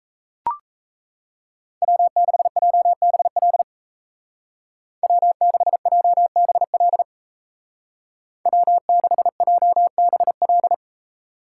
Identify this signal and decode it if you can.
Morse code, heard as a keyed tone, about 34 wpm, 700 Hz. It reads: W6JBL W6JBL W6JBL
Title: Speed Racing - USA 1x3 Call Signs 28wpm